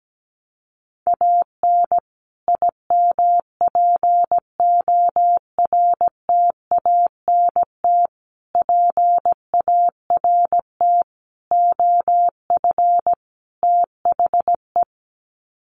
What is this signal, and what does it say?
Morse code, heard as a keyed tone, about 17 wpm, 700 Hz. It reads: AN IMPORTANT PART OF THE
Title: Sets of 5 Words Encore 17wpm